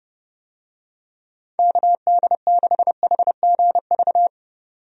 Morse code, heard as a keyed tone, about 30 wpm, 700 Hz. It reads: KD6HGV
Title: USA Call Signs - 6 Characters - 30wpm